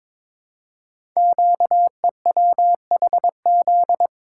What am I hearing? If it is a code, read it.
Morse code, heard as a keyed tone, about 22 wpm, 700 Hz. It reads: QEWHZ
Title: Code Groups - 5 Letter:Numbers - 22wpm